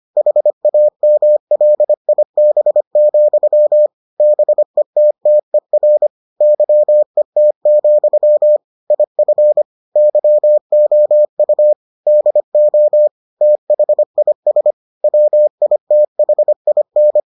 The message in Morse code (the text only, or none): HAMLIB, BETTER YET, IF YOU DO THIS WITHIN